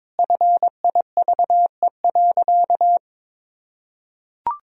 Morse code, heard as a keyed tone, about 22 wpm, 700 Hz. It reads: FIVE.
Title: Sentences from Top 1000 Words - Review Part 2 - 22wpm